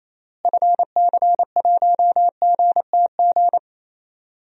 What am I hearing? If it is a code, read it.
Morse code, heard as a keyed tone, about 27 wpm, 700 Hz. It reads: FC1GTG
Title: DXCC-100 Call Signs - 6 Characters - Repeated 3x - 28wpm